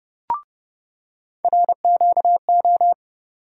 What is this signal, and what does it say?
Morse code, heard as a keyed tone, about 30 wpm, 700 Hz. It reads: RQO
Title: USA Call Sign Postfixes - 3 Letters 30wpm